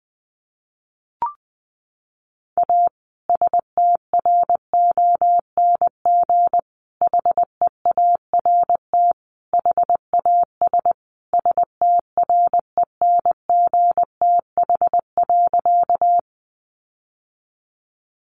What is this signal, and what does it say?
Morse code, heard as a keyed tone, about 20 wpm, 700 Hz. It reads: A STRONG HEART HAS STRENGTH.
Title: Sentences from Top 1300 Words 20wpm